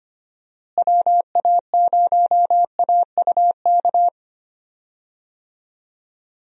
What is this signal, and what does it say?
Morse code, heard as a keyed tone, about 25 wpm, 700 Hz. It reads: WA0AUK
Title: USA Call Signs - 6 Characters - Repeated 2x - 25wpm